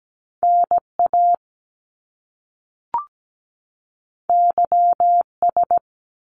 Morse code, heard as a keyed tone, about 17 wpm, 700 Hz. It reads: NA YS
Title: Top Two Letter Combinations in Words 17wpm